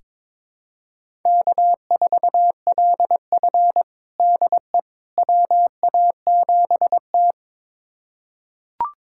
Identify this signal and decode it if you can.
Morse code, heard as a keyed tone, about 22 wpm, 700 Hz. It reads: K4LF DE WA7T
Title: USA Callsign DE Callsign - 4 Characters - 22wpm